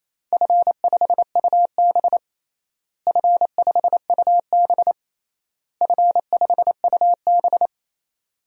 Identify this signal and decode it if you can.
Morse code, heard as a keyed tone, about 28 wpm, 700 Hz. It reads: F5UB F5UB F5UB
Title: DXCC-100 Call Signs - 4 Characters - Repeated 3x - 28wpm